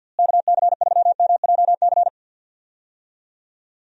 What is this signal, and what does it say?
Morse code, heard as a keyed tone, about 50 wpm, 700 Hz. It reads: KC3MJC
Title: USA Call Signs - 6 Characters - Repeated 3x - 50wpm